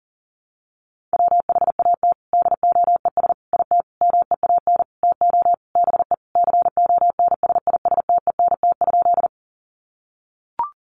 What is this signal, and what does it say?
Morse code, heard as a keyed tone, about 40 wpm, 700 Hz. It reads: WHAT DOES IT MEAN TO BE CONSISTENT?
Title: Sentences from Top 1500 Words 40wpm